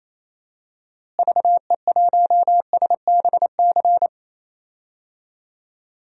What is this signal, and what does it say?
Morse code, heard as a keyed tone, about 27 wpm, 700 Hz. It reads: VE1SBC
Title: DXCC-100 Call Signs - 6 Characters - Repeated 2x - 28wpm